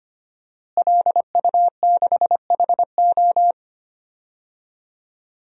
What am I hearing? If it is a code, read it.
Morse code, heard as a keyed tone, about 25 wpm, 700 Hz. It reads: LU6HO
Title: DXCC-100 Call Signs - 5 Characters - Repeated 3x - 25wpm